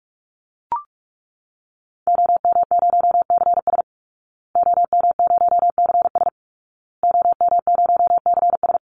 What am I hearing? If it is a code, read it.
Morse code, heard as a keyed tone, about 44 wpm, 700 Hz. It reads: OM0CS OM0CS OM0CS
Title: Call Signs Repeated 2x - 45wpm